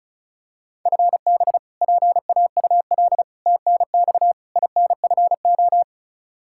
Morse code, heard as a keyed tone, about 34 wpm, 700 Hz. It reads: FB PAUL TNX INFO
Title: Practice QSO - Exchange 3 - More Info - 35wpm